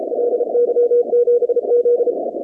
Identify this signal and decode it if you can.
Morse code, heard as a keyed tone, about 36 wpm, 470 Hz. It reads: LY7Z
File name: contests / 2019-cw-wpx-kt5j-k5pi- -2330 0-23